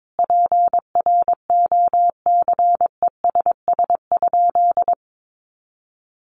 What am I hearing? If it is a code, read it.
Morse code, heard as a keyed tone, about 22 wpm, 700 Hz. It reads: PROCESS?
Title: Sentences from Top 1400 Words 22wpm